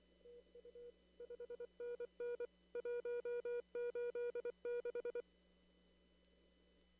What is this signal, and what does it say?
Morse code, heard as a keyed tone, about 24 wpm, 480 Hz. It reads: TU 5NN 186
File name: CW / kt5j wpx cw-op-k5pi